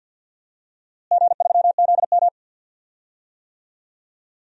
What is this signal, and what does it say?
Morse code, heard as a keyed tone, about 51 wpm, 700 Hz. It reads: G3ZM
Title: DXCC-100 Call Signs - 4 Characters - Repeated 3x - 50wpm